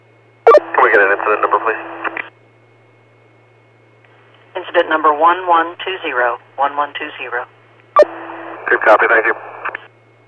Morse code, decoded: none

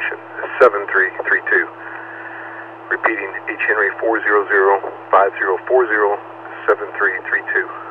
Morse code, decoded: none

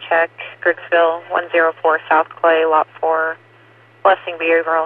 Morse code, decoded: none

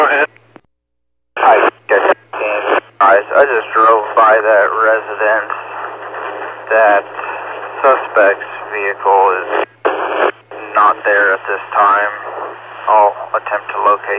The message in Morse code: none

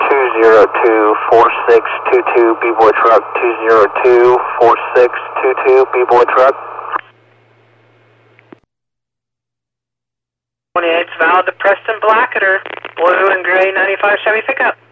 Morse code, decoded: none

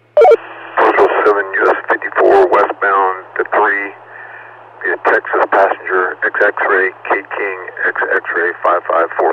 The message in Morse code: none

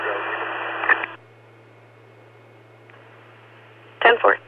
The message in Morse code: none

